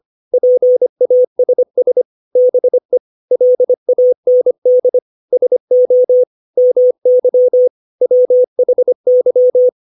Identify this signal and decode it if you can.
Morse code, heard as a keyed tone, about 25 wpm, 500 Hz. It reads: PASS BE LAND SO MY WHY
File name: 632 words / 632 Words 25WPM